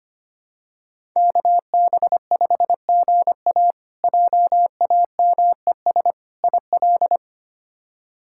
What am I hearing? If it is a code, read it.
Morse code, heard as a keyed tone, about 25 wpm, 700 Hz. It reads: KB5GA JAMES IL
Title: K1USN SST QSO Practice - 25wpm